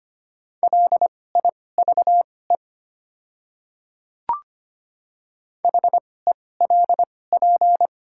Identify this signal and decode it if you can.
Morse code, heard as a keyed tone, about 25 wpm, spaced as for 18 wpm, 700 Hz. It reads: LIVE HELP
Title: 2x Character Spacing - Top 200 Words - 25wpm